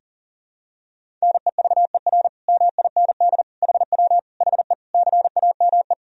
Morse code, heard as a keyed tone, about 40 wpm, 700 Hz. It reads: NEVER MIND HW HE CAME
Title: Sentences from Top 500 Words and Abbreviations 40wpm